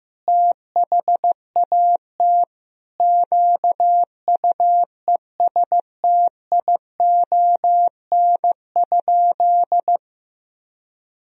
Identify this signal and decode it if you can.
Morse code, heard as a keyed tone, about 15 wpm, 700 Hz. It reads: THAT QUESTION?
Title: Sentences from Top 400 Words and Abbreviations 15wpm